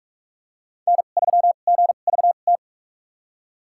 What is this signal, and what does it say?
Morse code, heard as a keyed tone, about 46 wpm, spaced as for 33 wpm, 700 Hz. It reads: N3GVT
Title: USA Call Signs - 5 Characters - 2x Character Spacing - 45wpm